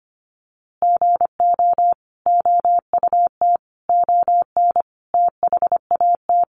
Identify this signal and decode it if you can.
Morse code, heard as a keyed tone, about 25 wpm, 700 Hz. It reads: GO OUT ON THAT
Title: A Sound Of Thunder 25wpm